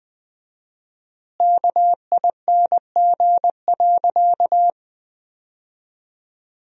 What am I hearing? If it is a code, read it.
Morse code, heard as a keyed tone, about 21 wpm, 700 Hz. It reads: KING.
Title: 4x Word-Spacing - Sentences from Top 400 Words - 20wpm